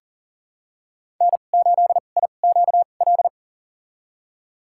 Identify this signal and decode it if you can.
Morse code, heard as a keyed tone, about 40 wpm, spaced as for 29 wpm, 700 Hz. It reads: N8IQL
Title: USA Call Signs - 2x Character Spacing - 40wpm